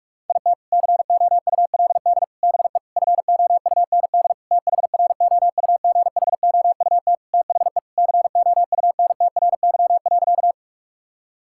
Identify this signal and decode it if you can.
Morse code, heard as a keyed tone, about 45 wpm, 700 Hz. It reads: IT COULD BE FOUND THROUGHOUT THE COUNTRY.
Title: Sentences from Top 1000 Words - Review Part 5 - 45wpm